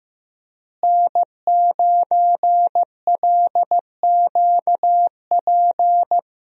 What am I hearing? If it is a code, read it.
Morse code, heard as a keyed tone, about 15 wpm, 700 Hz. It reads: N9LQP
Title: USA Call Signs - 5 Characters - Repeated 3x - 15wpm